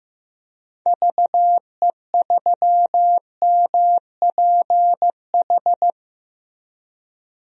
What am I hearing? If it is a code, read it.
Morse code, heard as a keyed tone, about 15 wpm, 700 Hz. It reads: VE3MPH
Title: DXCC-100 Call Signs - 6 Characters - Repeated 3x - 15wpm